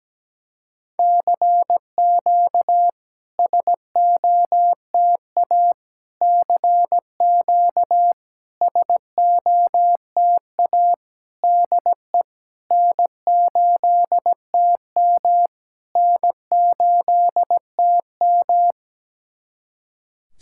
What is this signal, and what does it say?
Morse code, heard as a keyed tone, about 17 wpm, 700 Hz. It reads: CQ SOTA CQ SOTA DE N8TM N8TM
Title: SOTA QSO Practice 17wpm